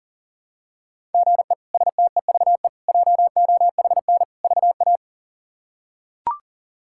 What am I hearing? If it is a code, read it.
Morse code, heard as a keyed tone, about 40 wpm, 700 Hz. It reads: GE STEVE JOHN VA
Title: K1USN SST QSO Practice - 40wpm